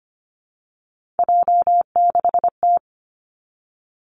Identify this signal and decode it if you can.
Morse code, heard as a keyed tone, about 25 wpm, 700 Hz. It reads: J6T
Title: Three Challenging Letter-Number Combinations 25wpm